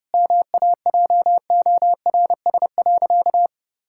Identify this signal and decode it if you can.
Morse code, heard as a keyed tone, about 30 wpm, 700 Hz. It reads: MAJORS.
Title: Pangram Challenge v2 - 30wpm